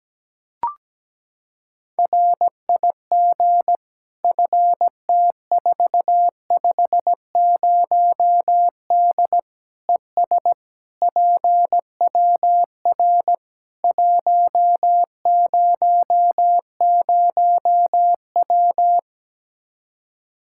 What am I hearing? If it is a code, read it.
Morse code, heard as a keyed tone, about 17 wpm, 700 Hz. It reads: RIG FT450D ES PWR 100W
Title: K1USN SST QSO Practice 17wpm